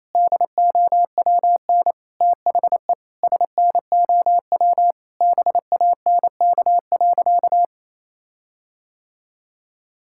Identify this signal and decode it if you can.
Morse code, heard as a keyed tone, about 28 wpm, 700 Hz. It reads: DOWN THE SNOW BANK.
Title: Sentences from Top 1600 Words 28wpm